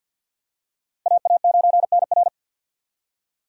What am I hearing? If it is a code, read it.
Morse code, heard as a keyed tone, about 53 wpm, 700 Hz. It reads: AA9NR
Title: USA Call Signs - 5 Characters - 4x Character Spacing - 50wpm